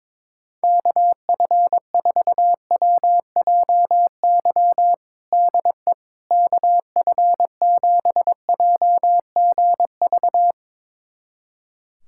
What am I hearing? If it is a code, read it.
Morse code, heard as a keyed tone, about 22 wpm, 700 Hz. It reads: KF4WJY DE KF7JGV
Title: USA Callsign DE Callsign 22wpm